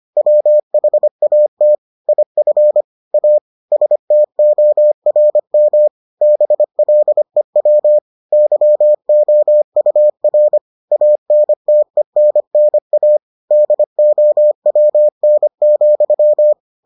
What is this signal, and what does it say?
Morse code, heard as a keyed tone, about 25 wpm, 600 Hz. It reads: WHAT IF A STORM BLEW YOUR ANTENNA DOWN,